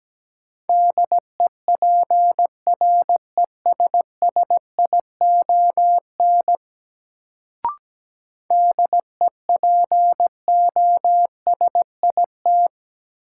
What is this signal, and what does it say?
Morse code, heard as a keyed tone, about 17 wpm, 700 Hz. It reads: DEPRESSION DEPOSIT
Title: Sentences from Top 1300 Words - New Word Drill - 17wpm